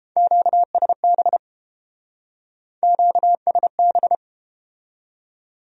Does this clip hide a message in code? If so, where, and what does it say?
Yes: Morse code from 0.2 to 4.2 s, heard as a keyed tone, about 30 wpm, 700 Hz. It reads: QSB QSB